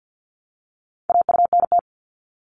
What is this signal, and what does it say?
Morse code, heard as a keyed tone, about 50 wpm, 700 Hz. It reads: AUNT
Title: Multi-letter Phonemes Repeated 3x 50wpm